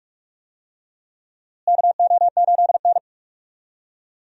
Morse code, heard as a keyed tone, about 41 wpm, 700 Hz. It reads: KO8N
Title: USA Call Signs - 4 Characters - Repeated 3x - 45wpm